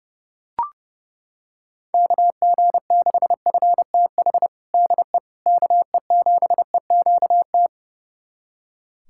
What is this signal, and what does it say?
Morse code, heard as a keyed tone, about 32 wpm, 700 Hz. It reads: KG6FTH DE KE7EQT
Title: USA Callsign DE Callsign - 6 Characters - 30wpm